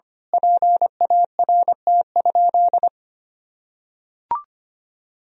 Morse code, heard as a keyed tone, about 25 wpm, 700 Hz. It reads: PART?
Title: Sentences from Top 1000 Words - Review Part 4 - 25wpm